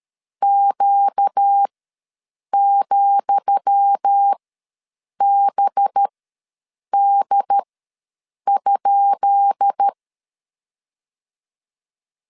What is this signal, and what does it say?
Morse code, heard as a keyed tone, about 13 wpm, spaced as for 7 wpm, 800 Hz. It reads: Q,BD?
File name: LEZIONI / 10 SIGNS / 10D COMA ws 12wpm 3618